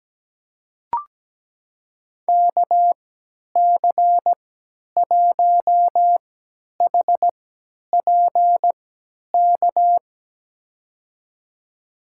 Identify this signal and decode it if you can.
Morse code, heard as a keyed tone, about 17 wpm, spaced as for 10 wpm, 700 Hz. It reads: KC1HPK
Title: USA Call Signs - 6 Characters - 3x Character Spacing - 17wpm